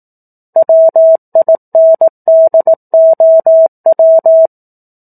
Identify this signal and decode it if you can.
Morse code, heard as a keyed tone, about 18 wpm, 650 Hz. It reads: WINDOW
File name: 04-Chapter-04 18 WPM